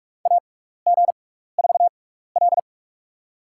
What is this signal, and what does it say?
Morse code, heard as a keyed tone, about 44 wpm, spaced as for 15 wpm, 700 Hz. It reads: AG4L
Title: USA Call Signs - 4 Characters - 6x Character Spacing - 45wpm